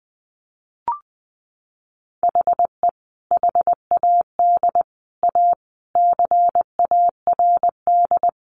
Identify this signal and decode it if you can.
Morse code, heard as a keyed tone, about 20 wpm, 700 Hz. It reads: HE HAD A CARD
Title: Sentences from Top 600 Words 20wpm